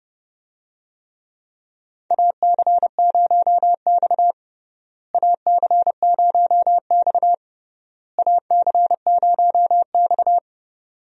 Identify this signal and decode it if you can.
Morse code, heard as a keyed tone, about 32 wpm, 700 Hz. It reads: AC0X AC0X AC0X
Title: USA Call Signs - 4 Characters - Repeated 3x - 30wpm